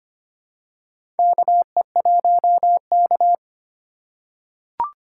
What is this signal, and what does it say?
Morse code, heard as a keyed tone, about 25 wpm, 700 Hz. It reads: KE1K
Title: USA Call Signs - 4 Characters - 3x Character Spacing - 25wpm